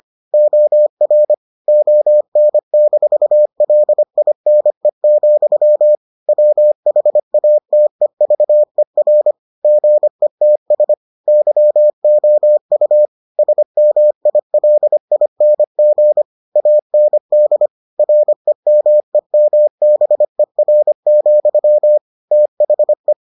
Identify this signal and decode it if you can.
Morse code, heard as a keyed tone, about 25 wpm, 600 Hz. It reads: OR ON-LINE, WHATEVER GETS YOU SMILING AND REMEMBER, THE